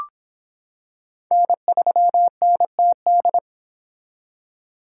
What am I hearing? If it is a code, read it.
Morse code, heard as a keyed tone, about 25 wpm, 700 Hz. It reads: N3NTD